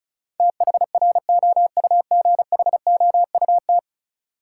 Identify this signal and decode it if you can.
Morse code, heard as a keyed tone, about 34 wpm, 700 Hz. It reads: THROUGHOUT
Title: Sentences from Top 800 Words - New Word Drill - 35wpm